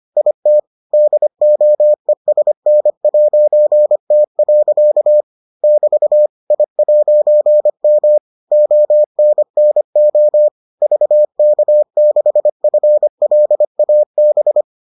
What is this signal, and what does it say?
Morse code, heard as a keyed tone, about 25 wpm, 600 Hz. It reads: IT DOESN'T. <BT> I'M ONNO VK6FLAB